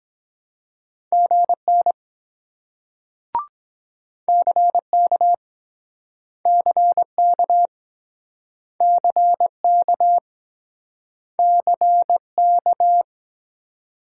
Morse code, heard as a keyed tone, about 24 wpm, 700 Hz. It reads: GN CK CK CK CK